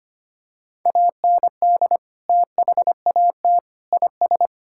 Morse code, heard as a keyed tone, about 25 wpm, 700 Hz. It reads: AND THAT IS
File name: practice / 25wpm / Sentences with 3-Word Phrases - 25wpm